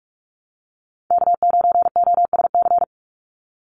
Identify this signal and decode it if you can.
Morse code, heard as a keyed tone, about 46 wpm, 690 Hz. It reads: K9OSC
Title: NA SOTA Chaser Callsigns 45wpm